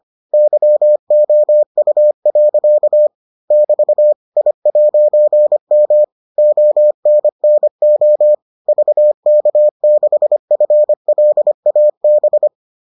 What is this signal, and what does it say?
Morse code, heard as a keyed tone, about 25 wpm, 600 Hz. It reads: YOU. <BT> I'M ONNO VK6FLAB